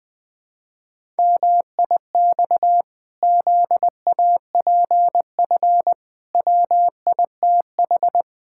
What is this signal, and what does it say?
Morse code, heard as a keyed tone, about 21 wpm, 700 Hz. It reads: MIX ZAPF WITH
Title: Pangram Challenge v2 - 20wpm